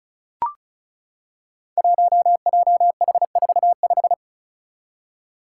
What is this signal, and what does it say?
Morse code, heard as a keyed tone, about 36 wpm, 700 Hz. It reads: 1JH45
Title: Five Challenging Letter-Number Combinations 35wpm